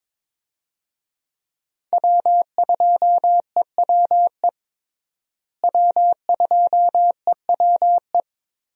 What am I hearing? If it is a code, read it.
Morse code, heard as a keyed tone, about 22 wpm, 700 Hz. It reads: W2EWE W2EWE
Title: USA Call Signs - 5 Characters - Repeated 3x - 22wpm